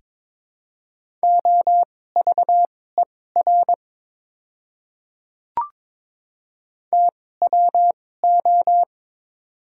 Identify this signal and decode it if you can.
Morse code, heard as a keyed tone, about 22 wpm, spaced as for 16 wpm, 700 Hz. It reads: OVER TWO
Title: 2x Character Spacing - Top 300 Words - 22wpm